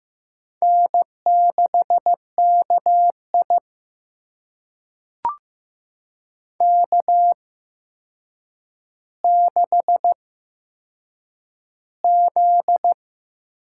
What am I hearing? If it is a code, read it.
Morse code, heard as a keyed tone, about 15 wpm, spaced as for 7 wpm, 700 Hz. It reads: N6KI K6Z
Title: USA Call Signs - 4 Characters - 8x Character Spacing - 15wpm